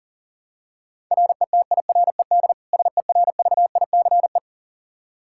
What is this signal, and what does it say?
Morse code, heard as a keyed tone, about 40 wpm, 700 Hz. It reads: RETIRED SERVICE
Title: Practice QSO - Exchange 3 - More Info - 40wpm